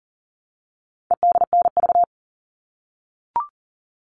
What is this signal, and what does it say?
Morse code, heard as a keyed tone, about 40 wpm, 700 Hz. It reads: EDNV
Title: All Cut Numbers - 4-Digit 40wpm